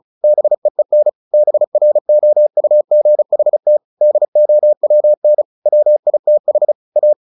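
Morse code, heard as a keyed tone, about 34 wpm, 600 Hz. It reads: BEEN BROUGHT DOWN WITH A
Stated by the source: Morse practice at 35 wpm from CW med Gnister, sending Ken Follett's novel Never in English.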